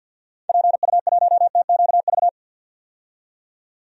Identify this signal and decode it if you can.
Morse code, heard as a keyed tone, about 50 wpm, 700 Hz. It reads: PU1TQV
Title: DXCC-100 Call Signs - 6 Characters - 50wpm